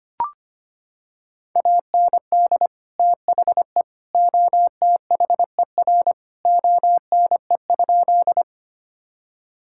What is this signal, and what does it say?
Morse code, heard as a keyed tone, about 25 wpm, 700 Hz. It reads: AND THE OTHER ONE?
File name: practice / 25wpm / Sentences with 3-Word Phrases - 25wpm